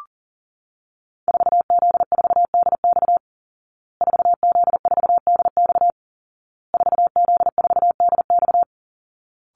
Morse code, heard as a keyed tone, about 40 wpm, 700 Hz. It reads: 4Z4DX 4Z4DX 4Z4DX
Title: Call Signs Repeated 2x - 40wpm